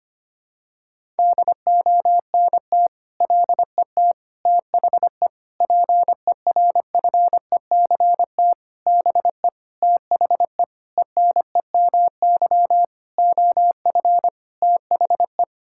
Morse code, heard as a keyed tone, about 25 wpm, 700 Hz. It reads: DONT LET THE PERFECT BE THE ENEMY OF THE
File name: practice / 25wpm / Sentences with 9-Word Phrases - 25wpm